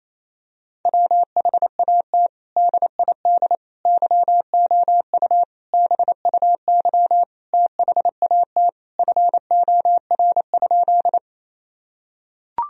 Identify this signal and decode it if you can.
Morse code, heard as a keyed tone, about 28 wpm, 700 Hz. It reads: WHAT DID YOU BUY THAT FOR?
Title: Sentences from Top 1000 Words - Review Part 4 - 28wpm